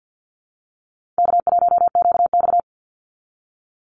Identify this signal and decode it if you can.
Morse code, heard as a keyed tone, about 50 wpm, 700 Hz. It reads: K1QX
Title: Easy Call Signs - 50wpm